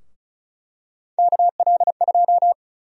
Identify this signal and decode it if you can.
Morse code, heard as a keyed tone, about 36 wpm, 700 Hz. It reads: KL2
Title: USA Call Sign Prefixes - 35wpm